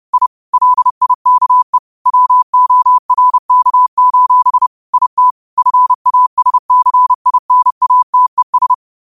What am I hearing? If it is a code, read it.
Morse code, heard as a keyed tone, about 30 wpm, 1000 Hz. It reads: I LIKE WORK: IT FASCINATES
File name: 30wpm